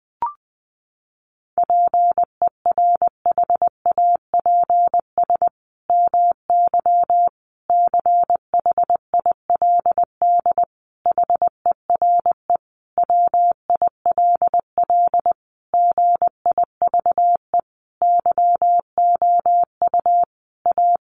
Morse code, heard as a keyed tone, about 20 wpm, 700 Hz. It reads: PERHAPS MY CHILD HERE WILL GIVE YOU A
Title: Sentences from Top 1100 Words 20wpm